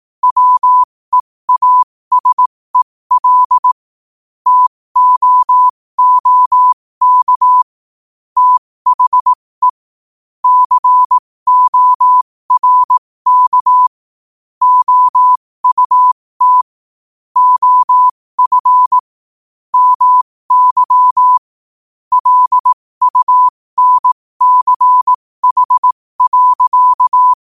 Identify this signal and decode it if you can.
Morse code, heard as a keyed tone, about 18 wpm, spaced as for 15 wpm, 1000 Hz. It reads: WEASEL TOOK THE CORK OUT OF MY LUNCH.
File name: qotdfiles / 15wpm